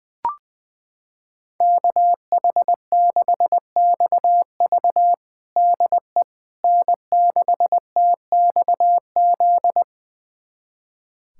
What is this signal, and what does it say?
Morse code, heard as a keyed tone, about 20 wpm, 700 Hz. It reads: KH6XV DE N6TXZ
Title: USA Callsign DE Callsign - 5 Characters - 20wpm